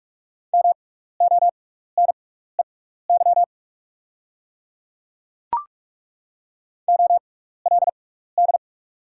Morse code, heard as a keyed tone, about 43 wpm, spaced as for 15 wpm, 700 Hz. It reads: MONEY OLD
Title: 6x Character Spacing - Top 500 Words - 45wpm